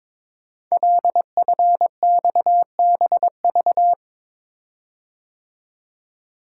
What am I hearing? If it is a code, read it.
Morse code, heard as a keyed tone, about 22 wpm, 700 Hz. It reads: LFXBV
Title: Code Groups - 5 Letters - 22wpm